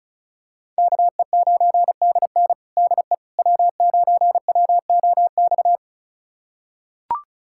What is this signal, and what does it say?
Morse code, heard as a keyed tone, about 34 wpm, 690 Hz. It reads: KE9DN DE W9WOX
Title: USA Callsign DE Callsign - 5 Characters - 35wpm